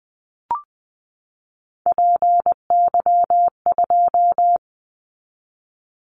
Morse code, heard as a keyed tone, about 20 wpm, 700 Hz. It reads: PY2
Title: Callsign Prefixes for Top 100 DXCC - 20wpm